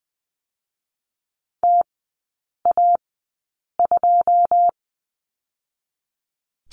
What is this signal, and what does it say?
Morse code, heard as a keyed tone, about 20 wpm, spaced as for 8 wpm, 700 Hz. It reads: TA2
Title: 087 Lesson 15e - Number 5 - Three Characters 20wpm